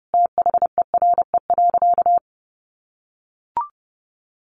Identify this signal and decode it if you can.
Morse code, heard as a keyed tone, about 30 wpm, 700 Hz. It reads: THERE.
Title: Sentences from Top 600 Words 30wpm